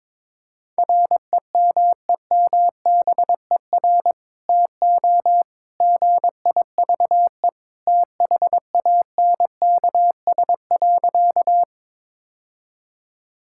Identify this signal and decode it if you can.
Morse code, heard as a keyed tone, about 22 wpm, 700 Hz. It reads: REMEMBER TO GIVE THANKS.
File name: practice / 22wpm / Sentences with 2-Word Phrases - 22wpm